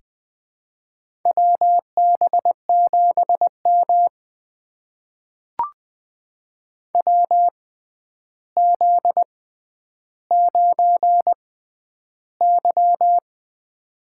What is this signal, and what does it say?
Morse code, heard as a keyed tone, about 21 wpm, spaced as for 10 wpm, 700 Hz. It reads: WB7M WZ9Y
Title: USA Call Signs - 4 Characters - 6x Character Spacing - 20wpm